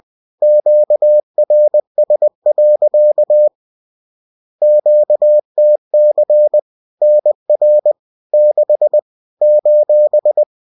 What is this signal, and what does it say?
Morse code, heard as a keyed tone, about 20 wpm, 600 Hz. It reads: QRS. QTC NR 6 :